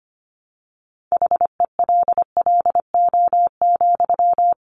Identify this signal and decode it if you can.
Morse code, heard as a keyed tone, about 25 wpm, 700 Hz. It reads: HELLO,
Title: Sentences from Top 1300 Words 25wpm